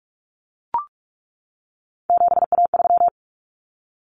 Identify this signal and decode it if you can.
Morse code, heard as a keyed tone, about 44 wpm, 700 Hz. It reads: 7A3
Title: Common Cut Numbers - 3-Digit 45wpm